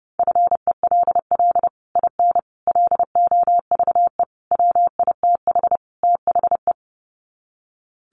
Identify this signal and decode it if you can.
Morse code, heard as a keyed tone, about 30 wpm, 700 Hz. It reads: FELL IN LOVE WITH THE
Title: Sets of 5 Words Encore2 30wpm